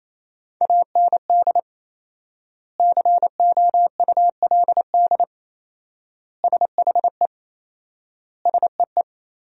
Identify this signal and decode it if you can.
Morse code, heard as a keyed tone, about 27 wpm, 700 Hz. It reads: AND COULD SHE SEE
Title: 4x Word-Spacing - Sentences from Top 100 Words - 28wpm